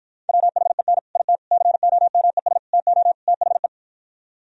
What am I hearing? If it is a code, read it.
Morse code, heard as a keyed tone, about 52 wpm, 700 Hz. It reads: WHEN IT COMES TO THE